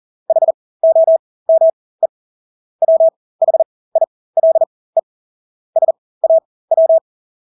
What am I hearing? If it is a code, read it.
Morse code, heard as a keyed tone, about 40 wpm, spaced as for 20 wpm, 650 Hz. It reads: HOME WHILE SAW
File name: Engelske ord 40-20 wpm